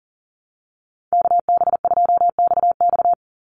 Morse code, heard as a keyed tone, about 41 wpm, 700 Hz. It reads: KB2XX
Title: NA SOTA Chaser Callsigns 40wpm